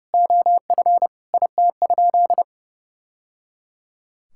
Morse code, heard as a keyed tone, about 30 wpm, 700 Hz. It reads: OF IT?
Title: Sentences from Top 1000 Words - Review Part 6 - 30wpm